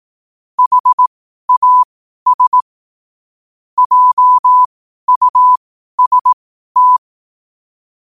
Morse code, heard as a keyed tone, about 18 wpm, spaced as for 12 wpm, 1000 Hz. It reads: HAS JUST
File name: qotdfiles / 12wpm